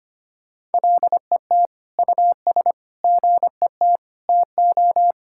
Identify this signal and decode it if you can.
Morse code, heard as a keyed tone, about 25 wpm, 700 Hz. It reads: LET US GET TO
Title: Sentences from Top 1000 Words - Review Part 6 - 25wpm